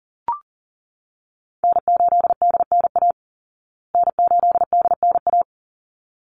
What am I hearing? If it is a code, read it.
Morse code, heard as a keyed tone, about 40 wpm, 700 Hz. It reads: N8DNA N8DNA
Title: Call Signs Repeated 3x - 40wpm